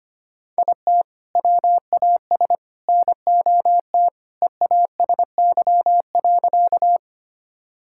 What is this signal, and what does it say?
Morse code, heard as a keyed tone, about 25 wpm, 700 Hz. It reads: IT WAS NOT EASY.
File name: practice / 25wpm / Sentences with 3-Word Phrases - 25wpm